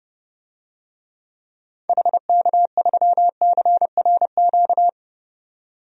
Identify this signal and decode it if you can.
Morse code, heard as a keyed tone, about 30 wpm, 700 Hz. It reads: HK3CRQ
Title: DXCC-100 Call Signs - 6 Characters - 30wpm